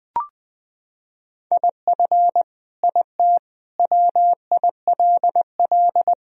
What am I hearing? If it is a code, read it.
Morse code, heard as a keyed tone, about 20 wpm, 700 Hz. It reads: IF IT WILL
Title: Sentences from Top 1000 Words - Review Part 2 - 20wpm